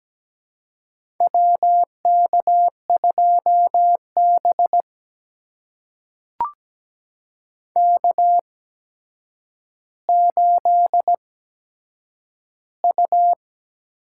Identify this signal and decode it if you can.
Morse code, heard as a keyed tone, about 17 wpm, spaced as for 8 wpm, 700 Hz. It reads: WK2B K8U
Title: USA Call Signs - 4 Characters - 8x Character Spacing - 17wpm